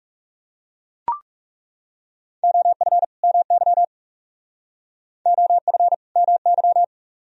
Morse code, heard as a keyed tone, about 37 wpm, 700 Hz. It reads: OF MY OF MY
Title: Speed-Racing - Sets of 2 Words 30wpm